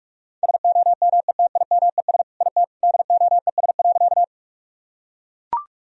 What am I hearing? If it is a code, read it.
Morse code, heard as a keyed tone, about 41 wpm, 700 Hz. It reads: SOMETIMES IT DOES.
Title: Sentences from Top 1000 Words - Review Part 3 - 45wpm